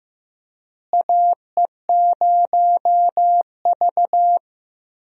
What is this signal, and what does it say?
Morse code, heard as a keyed tone, about 15 wpm, 700 Hz. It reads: AE0V
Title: USA Call Signs - 4 Characters - Repeated 3x - 15wpm